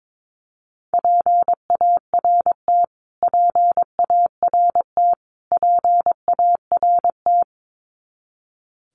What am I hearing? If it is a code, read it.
Morse code, heard as a keyed tone, about 22 wpm, 700 Hz. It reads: PART PART PART
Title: Top 500 Words Repeated 22wpm